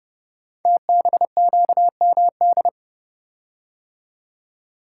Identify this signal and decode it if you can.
Morse code, heard as a keyed tone, about 30 wpm, 700 Hz. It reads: TBQMD
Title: Code Groups - 5 Letters - 30wpm